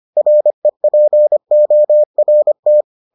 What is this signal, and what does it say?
Morse code, heard as a keyed tone, about 25 wpm, 600 Hz. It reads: REPORT